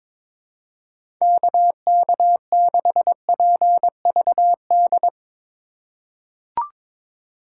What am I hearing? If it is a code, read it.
Morse code, heard as a keyed tone, about 22 wpm, 700 Hz. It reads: KK6PVD
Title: USA Call Signs - 6 Characters - 3x Character Spacing - 22wpm